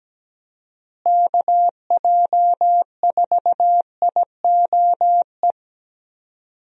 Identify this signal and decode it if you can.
Morse code, heard as a keyed tone, about 17 wpm, 700 Hz. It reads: KJ4IOE
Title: USA Call Signs - 6 Characters - Repeated 2x - 17wpm